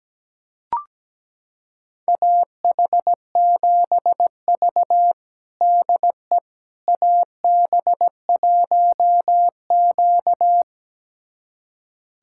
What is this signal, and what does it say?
Morse code, heard as a keyed tone, about 17 wpm, 700 Hz. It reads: AH7V DE AB1Q
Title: USA Callsign DE Callsign - 4 Characters - 17wpm